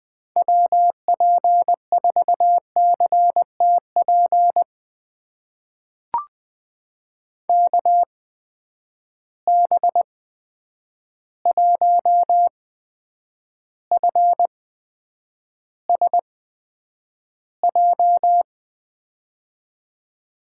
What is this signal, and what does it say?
Morse code, heard as a keyed tone, about 20 wpm, spaced as for 9 wpm, 700 Hz. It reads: WP4CTP KB1FSJ
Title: USA Call Signs - 6 Characters - 8x Character Spacing - 20wpm